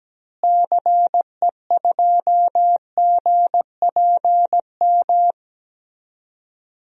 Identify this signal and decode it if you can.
Morse code, heard as a keyed tone, about 17 wpm, 700 Hz. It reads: CE2GPM
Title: DXCC-100 Call Signs - 6 Characters - Repeated 3x - 17wpm